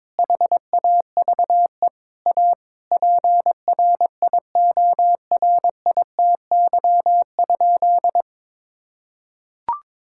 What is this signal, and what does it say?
Morse code, heard as a keyed tone, about 22 wpm, 700 Hz. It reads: HAVE A PRIORITY?
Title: Sentences from Top 1600 Words 22wpm